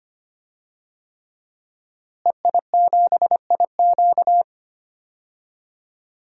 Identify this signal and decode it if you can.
Morse code, heard as a keyed tone, about 25 wpm, 700 Hz. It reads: EI7IQ
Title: DXCC-100 Call Signs - 5 Characters - Repeated 3x - 25wpm